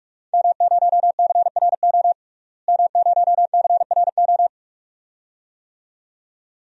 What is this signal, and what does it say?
Morse code, heard as a keyed tone, about 42 wpm, 700 Hz. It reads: M0CRO M0CRO
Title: DXCC-100 Call Signs - 5 Characters - Repeated 2x - 45wpm